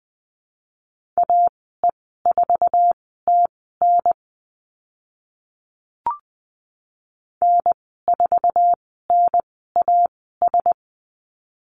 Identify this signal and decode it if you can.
Morse code, heard as a keyed tone, about 20 wpm, spaced as for 14 wpm, 700 Hz. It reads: AE4TN N4NAS
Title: 048 Lesson 9g - Number 4 - Call Signs 20wpm